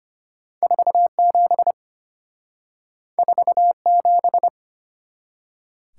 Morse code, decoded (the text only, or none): 47 47